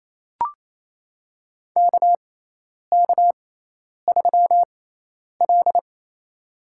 Morse code, heard as a keyed tone, about 28 wpm, spaced as for 10 wpm, 700 Hz. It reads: KK3L
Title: USA Call Signs - 6x Character Spacing - 28wpm